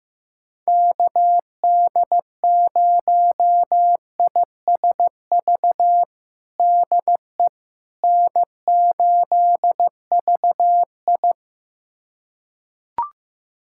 Morse code, heard as a keyed tone, about 15 wpm, 700 Hz. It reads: KD0ISV DE N8VI
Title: USA Callsign DE Callsign 15wpm